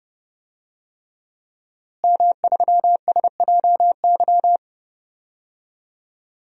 Morse code, heard as a keyed tone, about 32 wpm, 700 Hz. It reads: M3SJY
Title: DXCC-100 Call Signs - 5 Characters - Repeated 3x - 30wpm